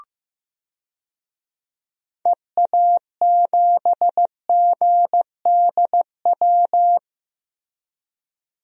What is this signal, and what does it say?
Morse code, heard as a keyed tone, about 15 wpm, 700 Hz. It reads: EA7GDW
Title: DXCC-100 Call Signs - 6 Characters - Repeated 2x - 15wpm